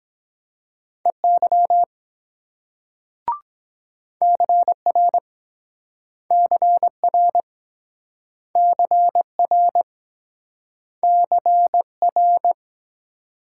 Morse code, heard as a keyed tone, about 23 wpm, 700 Hz. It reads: EY CR CR CR CR